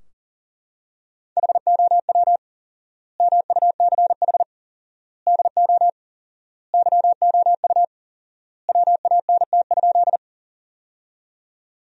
Morse code, heard as a keyed tone, about 40 wpm, 700 Hz. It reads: HOW MUCH DO YOU WANT?
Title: 4x Word-Spacing - Sentences from Top 200 Words - 40wpm